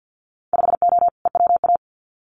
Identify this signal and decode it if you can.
Morse code, heard as a keyed tone, about 50 wpm, 700 Hz. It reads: 5O EWA
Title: Winter Field Day Exchanges Encore 50wpm